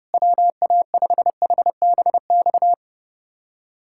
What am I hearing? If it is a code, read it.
Morse code, heard as a keyed tone, about 30 wpm, 700 Hz. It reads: WA5HBX
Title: USA Callsign DE Callsign 30wpm